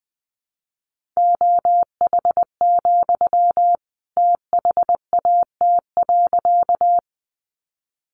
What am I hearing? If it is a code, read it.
Morse code, heard as a keyed tone, about 20 wpm, 700 Hz. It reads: OH, THAT.
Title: Sentences from Top 500 Words 20wpm